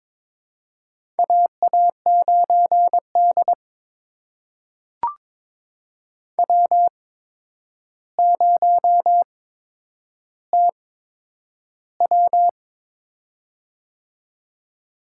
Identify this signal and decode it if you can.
Morse code, heard as a keyed tone, about 22 wpm, spaced as for 9 wpm, 700 Hz. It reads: AA9D W0TW
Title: USA Call Signs - 4 Characters - 8x Character Spacing - 22wpm